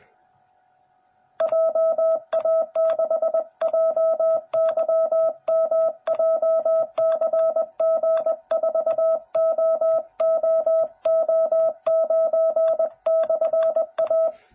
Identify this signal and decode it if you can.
Morse code, heard as a keyed tone, about 20 wpm, 660 Hz. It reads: JA6JYM JCG4OOO9/A
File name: jh8cbh / 2021cw / cw09b